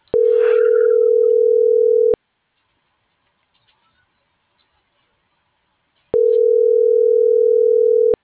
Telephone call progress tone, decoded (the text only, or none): ringback tone